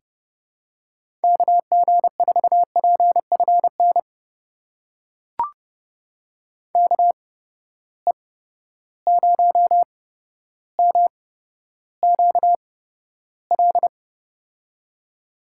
Morse code, heard as a keyed tone, about 31 wpm, spaced as for 13 wpm, 700 Hz. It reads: KG4PFN KE0MQL
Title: USA Call Signs - 6 Characters - 8x Character Spacing - 30wpm